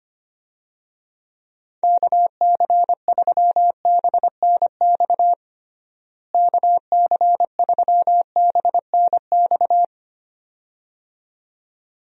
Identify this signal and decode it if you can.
Morse code, heard as a keyed tone, about 25 wpm, 700 Hz. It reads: KC3BNX KC3BNX
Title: USA Call Signs - 6 Characters - Repeated 2x - 25wpm